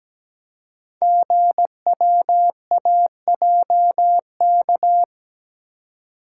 Morse code, heard as a keyed tone, about 17 wpm, 690 Hz. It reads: GWAJK
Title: Code Groups - 5 Letter:Numbers - 17wpm